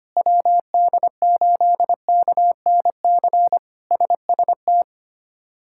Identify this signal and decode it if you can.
Morse code, heard as a keyed tone, about 25 wpm, 700 Hz. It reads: WD8KNC SST
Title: K1USN SST QSO Practice - 25wpm